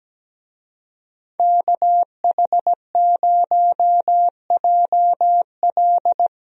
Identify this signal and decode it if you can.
Morse code, heard as a keyed tone, about 17 wpm, 700 Hz. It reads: KH0JL
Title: USA Call Signs - 5 Characters - 17wpm